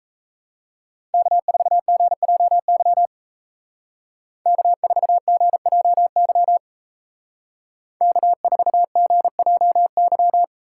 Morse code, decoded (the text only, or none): K4GJY K4GJY K4GJY